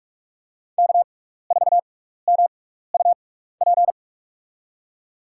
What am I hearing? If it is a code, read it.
Morse code, heard as a keyed tone, about 46 wpm, spaced as for 16 wpm, 700 Hz. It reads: K4MUP
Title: USA Call Signs - 6x Character Spacing - 45wpm